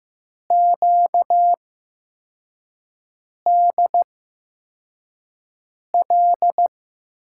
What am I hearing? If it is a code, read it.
Morse code, heard as a keyed tone, about 15 wpm, spaced as for 4 wpm, 700 Hz. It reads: QDL
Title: USA Call Signs - 5 Characters - 8x Character Spacing - 15wpm